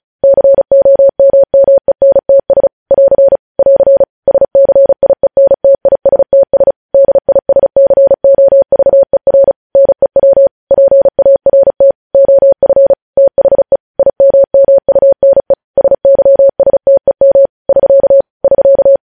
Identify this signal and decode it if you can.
Morse code, heard as a keyed tone, about 36 wpm, 550 Hz. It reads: COMMENTS <AR> <AR> SCIENTISTS DISCOVER NEW PART OF THE IMMUNE SYSTEM <SK> <SK>
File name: podcast / News headlines-1x-35x35